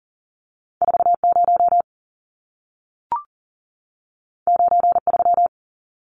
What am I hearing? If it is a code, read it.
Morse code, heard as a keyed tone, about 40 wpm, 700 Hz. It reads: 40 93
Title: Numbers - 2-Digit 40wpm